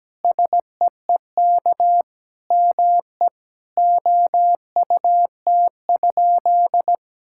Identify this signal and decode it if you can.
Morse code, heard as a keyed tone, about 17 wpm, 700 Hz. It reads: SEEK ME OUT?
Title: Sentences from Top 1600 Words 17wpm